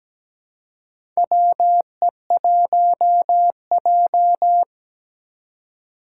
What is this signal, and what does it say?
Morse code, heard as a keyed tone, about 17 wpm, 700 Hz. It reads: WE1J
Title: USA Call Signs - 4 Characters - 2x Character Spacing - 17wpm